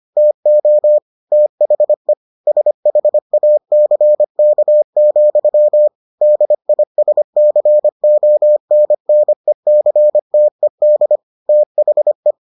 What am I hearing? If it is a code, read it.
Morse code, heard as a keyed tone, about 25 wpm, 600 Hz. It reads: TO THE SHACK, DISCONNECTED THE